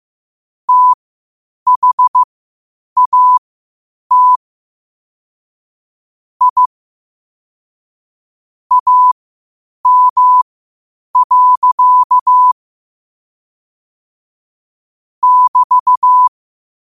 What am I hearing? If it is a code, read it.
Morse code, heard as a keyed tone, about 15 wpm, spaced as for 8 wpm, 1000 Hz. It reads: THAT I AM. <BT>
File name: qotdfiles / 8wpm